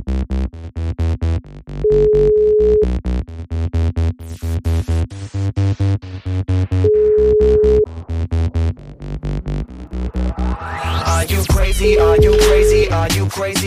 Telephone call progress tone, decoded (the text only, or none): ringback tone